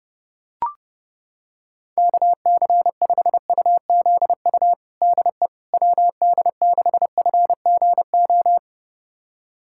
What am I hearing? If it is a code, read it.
Morse code, heard as a keyed tone, about 32 wpm, 700 Hz. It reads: KC5UZU DE WD6FGO
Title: USA Callsign DE Callsign 30wpm